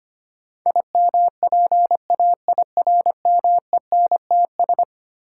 Morse code, heard as a keyed tone, about 25 wpm, 700 Hz. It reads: IMPAIRMENTS
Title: Sets of 5 Words Encore - New Word Drill - 25wpm